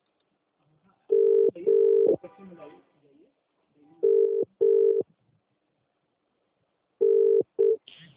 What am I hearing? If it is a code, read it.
Telephone call progress tone (UK), heard as a telephone network's tone, about 400, 450 Hz, with speech. Ringback tone.